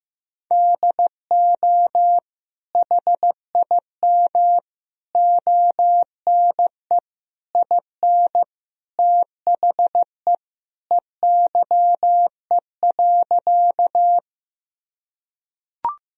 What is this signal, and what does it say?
Morse code, heard as a keyed tone, about 15 wpm, 700 Hz. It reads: DO HIM ONE IN THE EYE.
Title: Sentences from Top 1000 Words - Review Part 1 - 15wpm